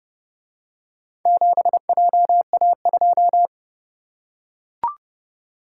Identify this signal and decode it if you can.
Morse code, heard as a keyed tone, about 32 wpm, 700 Hz. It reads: 7JA2